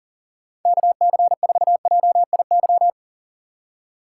Morse code, heard as a keyed tone, about 40 wpm, 700 Hz. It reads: KC4JIY
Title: USA Call Signs - 4x Character Spacing - 40wpm